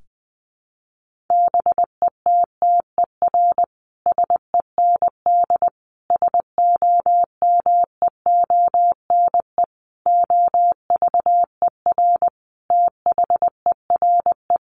Morse code, heard as a keyed tone, about 20 wpm, 700 Hz. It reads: BETTER SEND SOMEONE OVER THERE
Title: Sentences from Top 1500 Words 20wpm